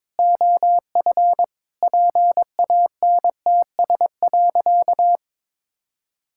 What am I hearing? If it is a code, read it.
Morse code, heard as a keyed tone, about 23 wpm, 700 Hz. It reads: OF PANTS.
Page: morse-code-ninja-practice-22wpm